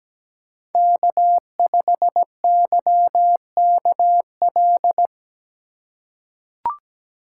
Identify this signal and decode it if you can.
Morse code, heard as a keyed tone, about 17 wpm, 700 Hz. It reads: K5YKL
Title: USA Call Signs - 5 Characters - 2x Character Spacing - 17wpm